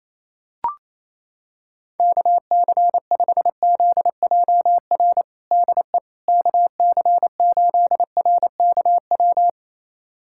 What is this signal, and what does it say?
Morse code, heard as a keyed tone, about 28 wpm, 700 Hz. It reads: KC5ZJR DE KC8RKW
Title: USA Callsign DE Callsign - 6 Characters - 28wpm